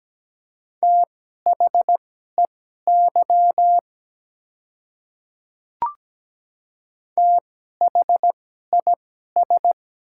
Morse code, heard as a keyed tone, about 17 wpm, spaced as for 12 wpm, 700 Hz. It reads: THEY THIS
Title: 2x Character Spacing - Top 100 Words - 17wpm